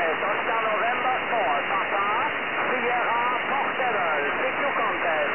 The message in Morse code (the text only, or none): none